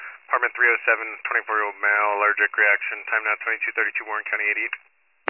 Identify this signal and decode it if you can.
Emergency Alert System (SAME) tones (header bursts), heard as harsh buzzing bursts, with speech. It is heard from 0.5 to 4.8 s.